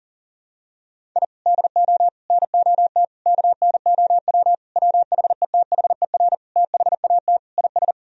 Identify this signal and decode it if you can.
Morse code, heard as a keyed tone, about 40 wpm, 700 Hz. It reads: I DO NOT KNOW WHETHER THAT IS
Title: Sentences from Top 1000 Words - Review Part 3 - 40wpm